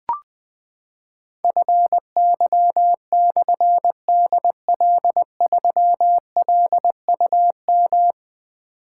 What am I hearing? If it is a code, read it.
Morse code, heard as a keyed tone, about 20 wpm, 700 Hz. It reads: FY/DL3LUM
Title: Difficult Call Signs 20wpm